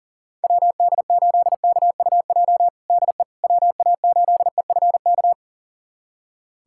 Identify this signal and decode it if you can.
Morse code, heard as a keyed tone, about 40 wpm, 700 Hz. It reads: WD8KUJ DE WA8EFK
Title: USA Callsign DE Callsign 40wpm